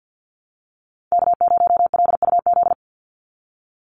Morse code, heard as a keyed tone, about 50 wpm, 700 Hz. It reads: K0LUZ
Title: Call Signs Repeated 2x - 50wpm